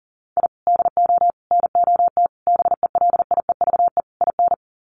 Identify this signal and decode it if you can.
Morse code, heard as a keyed tone, about 40 wpm, 700 Hz. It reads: I DO NOT BELIEVE IN
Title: Sets of 5 Words Encore 40wpm